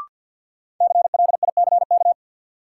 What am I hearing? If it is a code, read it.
Morse code, heard as a keyed tone, about 50 wpm, 700 Hz. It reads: CLICK
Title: Sentences from Top 1100 Words - New Word Drill - 50wpm